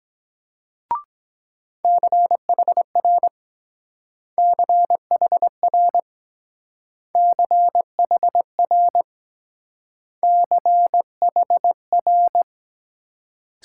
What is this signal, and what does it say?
Morse code, heard as a keyed tone, about 22 wpm, 700 Hz. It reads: CHR CHR CHR CHR